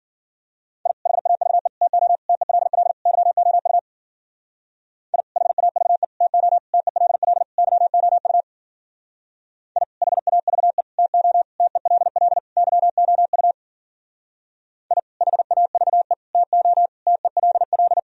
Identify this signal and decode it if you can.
Morse code, heard as a keyed tone, about 54 wpm, 700 Hz. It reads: I HAVE TO TELL YOU I HAVE TO TELL YOU I HAVE TO TELL YOU I HAVE TO TELL